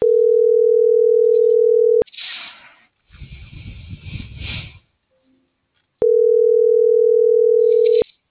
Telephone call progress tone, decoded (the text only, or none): ringback tone